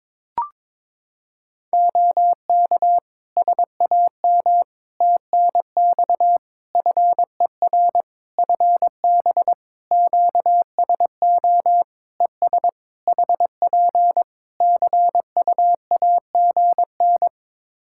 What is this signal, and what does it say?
Morse code, heard as a keyed tone, about 22 wpm, 700 Hz. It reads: OK SAM TNX FER FB QSO ES HP CUAGN
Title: Practice QSO - Exchange 4 - Ending - 22wpm